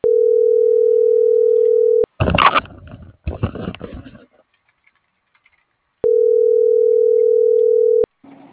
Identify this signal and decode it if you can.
Ringback tone (North America), 440 + 480 Hz.